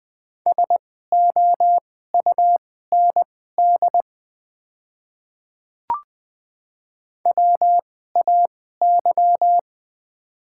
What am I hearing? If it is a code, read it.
Morse code, heard as a keyed tone, about 20 wpm, spaced as for 14 wpm, 700 Hz. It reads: SOUND WAY
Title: 2x Character Spacing - Top 100 Words - 20wpm